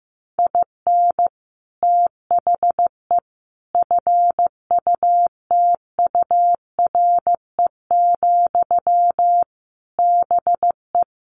Morse code, heard as a keyed tone, about 14 wpm, 700 Hz. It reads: IN THE FUTURE, BE
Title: Sentences from Top 1300 Words 15wpm